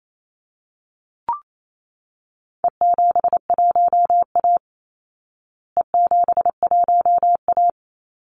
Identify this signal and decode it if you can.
Morse code, heard as a keyed tone, about 28 wpm, 700 Hz. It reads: E71A E71A
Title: Call Signs Repeated 3x - 28wpm